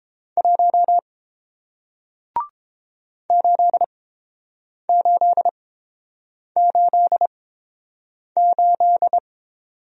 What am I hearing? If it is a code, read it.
Morse code, heard as a keyed tone, about 30 wpm, spaced as for 7 wpm, 700 Hz. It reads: 1 8888